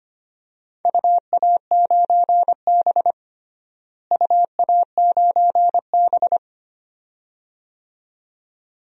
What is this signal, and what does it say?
Morse code, heard as a keyed tone, about 25 wpm, 700 Hz. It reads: UA9B UA9B
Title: DXCC-100 Call Signs - 4 Characters - Repeated 2x - 25wpm